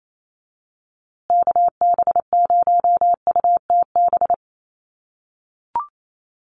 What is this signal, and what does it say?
Morse code, heard as a keyed tone, about 28 wpm, 690 Hz. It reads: KB0UTB
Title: USA Call Signs Repeated 3x 28wpm